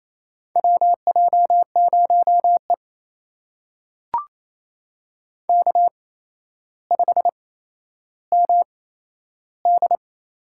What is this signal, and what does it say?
Morse code, heard as a keyed tone, about 28 wpm, spaced as for 12 wpm, 700 Hz. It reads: WJ0E K5MD
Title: USA Call Signs - 4 Characters - 8x Character Spacing - 28wpm